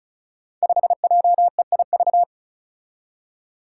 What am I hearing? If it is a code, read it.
Morse code, heard as a keyed tone, about 36 wpm, 700 Hz. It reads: 5JEIV
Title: Five Challenging Letter-Number Combinations 35wpm